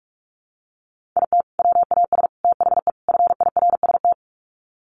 Morse code, heard as a keyed tone, about 44 wpm, 700 Hz. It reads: IT WAS THE FIRST
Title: Sets of 4 Words 45wpm